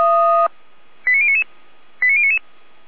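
Five-tone selective call (ZVEI), heard as quick runs of five tones, tones of about 70 ms. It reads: 89099 89099
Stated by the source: ZVEI five-tone selective call from a fire brigade, received on SDR.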